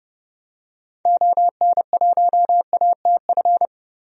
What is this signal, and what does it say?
Morse code, heard as a keyed tone, about 30 wpm, 700 Hz. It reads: ON1ATF
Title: DXCC-100 Call Signs - 6 Characters - Repeated 3x - 30wpm